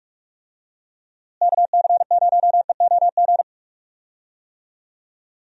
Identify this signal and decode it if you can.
Morse code, heard as a keyed tone, about 42 wpm, 700 Hz. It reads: KC0EOG
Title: USA Call Signs - 6 Characters - Repeated 3x - 45wpm